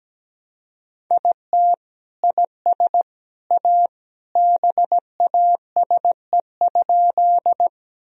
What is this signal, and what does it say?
Morse code, heard as a keyed tone, about 17 wpm, 700 Hz. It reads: IT IS A BASE?
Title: Sentences from Top 1000 Words - Review Part 1 - 17wpm